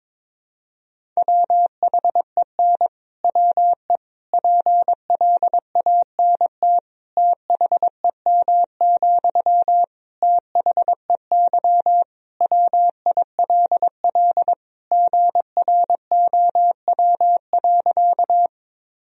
Morse code, heard as a keyed tone, about 22 wpm, 700 Hz. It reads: WHEN WE PLANT THEM, THEY WILL GROW.
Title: Sentences from Top 1000 Words - Review Part 1 - 22wpm